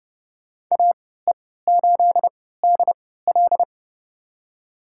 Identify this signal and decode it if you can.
Morse code, heard as a keyed tone, about 30 wpm, spaced as for 17 wpm, 700 Hz. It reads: AE8DL
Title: USA Call Signs - 5 Characters - 3x Character Spacing - 30wpm